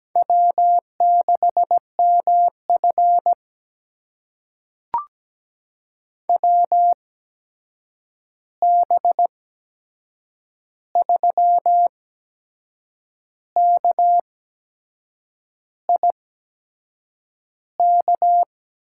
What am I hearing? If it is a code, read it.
Morse code, heard as a keyed tone, about 17 wpm, spaced as for 6 wpm, 700 Hz. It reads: W6MF WB3KIK
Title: USA Call Signs - 8x Character Spacing - 17wpm